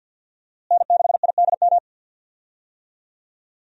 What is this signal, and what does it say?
Morse code, heard as a keyed tone, about 51 wpm, 700 Hz. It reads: N6IDM
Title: USA Call Signs - 5 Characters - Repeated 3x - 50wpm